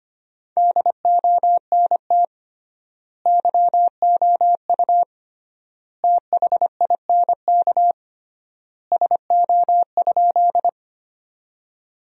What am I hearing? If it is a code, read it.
Morse code, heard as a keyed tone, about 25 wpm, 700 Hz. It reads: DONT YOU THINK SO?
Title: 3x Word-Spacing - Sentences from Top 300 Words - 25wpm